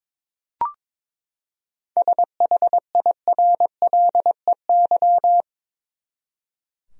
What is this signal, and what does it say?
Morse code, heard as a keyed tone, about 22 wpm, 700 Hz. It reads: SHIRLEY
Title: USA Common Names - 22wpm